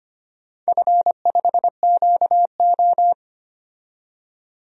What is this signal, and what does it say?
Morse code, heard as a keyed tone, about 25 wpm, 700 Hz. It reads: F5QO
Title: DXCC-100 Call Signs - 4 Characters - 25wpm